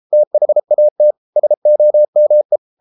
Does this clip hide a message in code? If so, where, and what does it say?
Yes: Morse code from 0.1 to 2.6 s, heard as a keyed tone, about 33 wpm, 600 Hz. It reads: THAT SOME